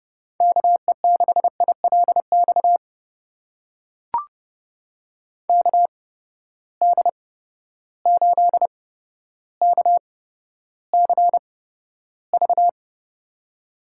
Morse code, heard as a keyed tone, about 30 wpm, spaced as for 13 wpm, 700 Hz. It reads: KE6ILX KD8KCV
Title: USA Call Signs - 6 Characters - 8x Character Spacing - 30wpm